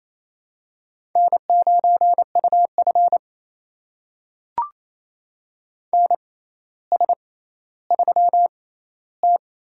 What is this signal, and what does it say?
Morse code, heard as a keyed tone, about 28 wpm, spaced as for 14 wpm, 700 Hz. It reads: N9UF NS3T
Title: USA Call Signs - 4 Characters - 6x Character Spacing - 28wpm